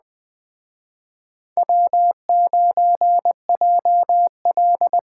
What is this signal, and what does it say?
Morse code, heard as a keyed tone, about 20 wpm, 700 Hz. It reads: W9JL
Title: Speed Racing - USA 1x2 Call Signs 17wpm